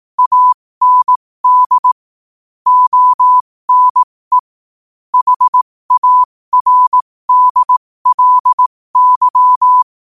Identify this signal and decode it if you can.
Morse code, heard as a keyed tone, about 18 wpm, spaced as for 15 wpm, 1000 Hz. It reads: AND ONE HARDLY
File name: qotdfiles / 15wpm